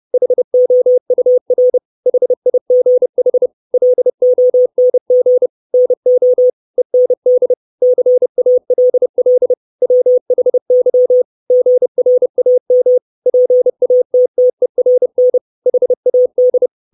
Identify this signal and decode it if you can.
Morse code, heard as a keyed tone, about 30 wpm, 500 Hz. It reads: HOUR HIGH LONG NO END CALL WHY GRAM PATTERN HAD
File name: 632 Words 30WPM